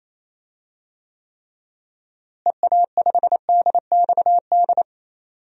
Morse code, heard as a keyed tone, about 27 wpm, 700 Hz. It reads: EA5DXD
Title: DXCC-100 Call Signs - 6 Characters - Repeated 3x - 28wpm